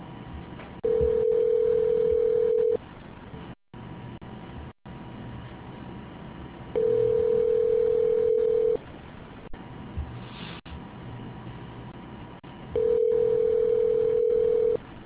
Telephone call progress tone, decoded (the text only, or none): ringback tone